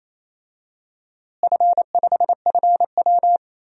Morse code, heard as a keyed tone, about 27 wpm, 700 Hz. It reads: F5FW
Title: DXCC-100 Call Signs - 4 Characters - Repeated 3x - 28wpm